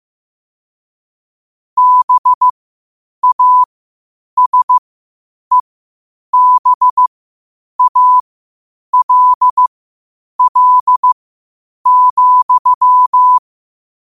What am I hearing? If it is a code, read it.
Morse code, heard as a keyed tone, about 15 wpm, spaced as for 8 wpm, 1000 Hz. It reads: BASEBALL,
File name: qotdfiles / 8wpm